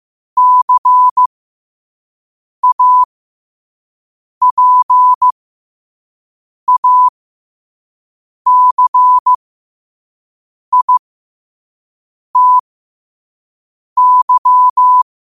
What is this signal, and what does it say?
Morse code, heard as a keyed tone, about 15 wpm, spaced as for 5 wpm, 1000 Hz. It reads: CAPACITY
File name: qotdfiles / 5wpm